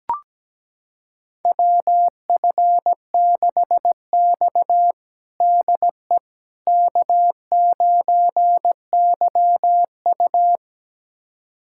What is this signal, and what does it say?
Morse code, heard as a keyed tone, about 17 wpm, 700 Hz. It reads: WF6X DE K9YU
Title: USA Callsign DE Callsign - 4 Characters - 17wpm